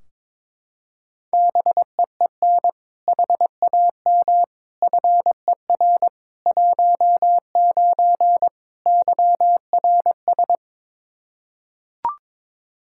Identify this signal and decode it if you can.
Morse code, heard as a keyed tone, about 22 wpm, 700 Hz. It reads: BEEN HAM FER 19 YRS
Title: Practice QSO - Exchange 3 - More Info - 22wpm